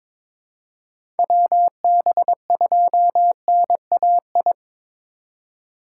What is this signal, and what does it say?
Morse code, heard as a keyed tone, about 22 wpm, 700 Hz. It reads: WB2NAI
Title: USA Call Signs - 6 Characters - 4x Character Spacing - 22wpm